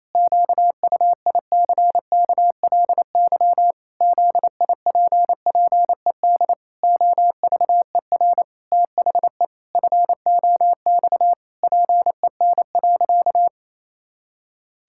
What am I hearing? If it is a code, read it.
Morse code, heard as a keyed tone, about 28 wpm, 700 Hz. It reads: QUICKLY ZIPPED OVER THE FOX PEN.
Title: Pangram Challenge v2 - 28wpm